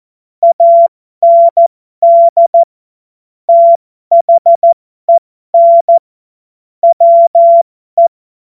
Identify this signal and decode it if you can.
Morse code, heard as a keyed tone, about 14 wpm, spaced as for 12 wpm, 680 Hz. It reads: AND THEN WE
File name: Morsekurs / Witze